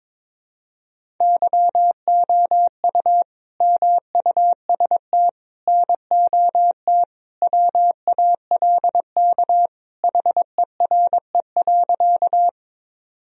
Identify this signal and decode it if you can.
Morse code, heard as a keyed tone, about 22 wpm, 700 Hz. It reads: YOU MUST NOT WALK HERE.
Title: Sentences from Top 1000 Words - Review Part 1 - 22wpm